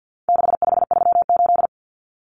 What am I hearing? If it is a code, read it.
Morse code, heard as a keyed tone, about 50 wpm, 700 Hz. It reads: 6538
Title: Common Cut Numbers - 4-Digit 50wpm